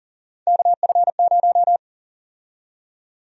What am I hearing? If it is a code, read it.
Morse code, heard as a keyed tone, about 39 wpm, 700 Hz. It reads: KF0
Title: USA Call Sign Prefixes - 40wpm